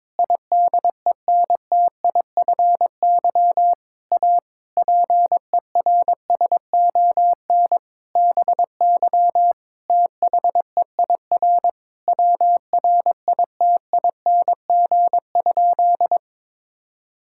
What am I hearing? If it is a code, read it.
Morse code, heard as a keyed tone, about 22 wpm, 700 Hz. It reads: IDENTIFY A PERSON BY THEIR WRITING?
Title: Sentences from Top 1000 Words - Review Part 6 - 22wpm